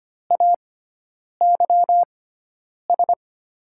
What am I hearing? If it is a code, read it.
Morse code, heard as a keyed tone, about 25 wpm, spaced as for 9 wpm, 690 Hz. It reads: AYS
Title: 6x Character Spacing - Top 300 Words - 25wpm